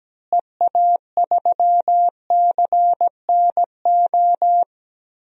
Morse code, heard as a keyed tone, about 17 wpm, 700 Hz. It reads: EA3CNO
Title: DXCC-100 Call Signs - 6 Characters - Repeated 2x - 17wpm